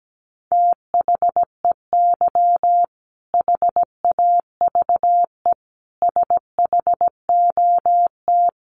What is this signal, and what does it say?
Morse code, heard as a keyed tone, about 17 wpm, 700 Hz. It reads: THEY HAVE SHOT
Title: Sentences from Top 1100 Words 17wpm